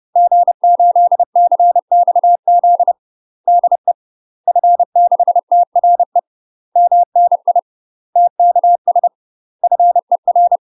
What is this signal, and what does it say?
Morse code, heard as a keyed tone, about 30 wpm, 700 Hz. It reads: G8CXZ DE F6TRE MNI TKS FER
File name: UK MASTER QSOs TRAINING / 30WPM / UK MASTER QSOs 0029